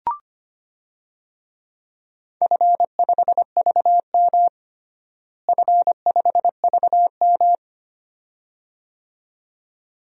Morse code, heard as a keyed tone, about 25 wpm, 700 Hz. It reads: F5VM F5VM
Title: DXCC-100 Call Signs - 4 Characters - Repeated 2x - 25wpm